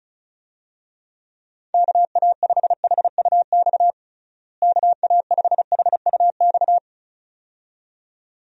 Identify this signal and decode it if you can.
Morse code, heard as a keyed tone, about 36 wpm, 700 Hz. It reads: KA5HUX KA5HUX
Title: USA Call Signs - 6 Characters - Repeated 2x - 35wpm